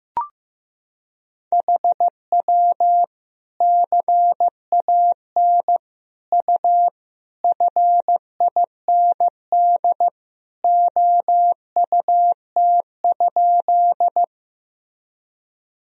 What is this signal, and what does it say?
Morse code, heard as a keyed tone, about 15 wpm, 700 Hz. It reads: HW CAN U FIND OUT?
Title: Sentences from Top 200 Words and Abbreviations 15wpm